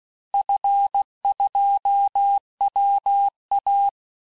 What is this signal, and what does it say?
Morse code, heard as a keyed tone, about 16 wpm, 800 Hz. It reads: F2WA
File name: cours-du-lundi files / cours-1592--teste-FIN-de-QSO-vitesse-15mpm